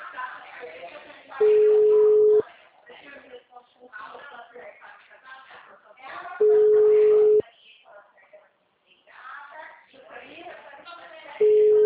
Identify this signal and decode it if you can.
Ringback tone (Europe, 425 Hz), with speech.